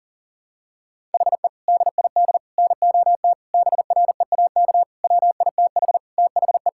Morse code, heard as a keyed tone, about 37 wpm, 700 Hz. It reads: HE DID NOT BREAK WITH THE
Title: Sentences from Top 1000 Words - Review Part 6 - 40wpm